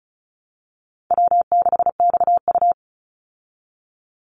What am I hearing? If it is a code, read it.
Morse code, heard as a keyed tone, about 34 wpm, 700 Hz. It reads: W6XU
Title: Easy Call Signs Repeated 3x - 35wpm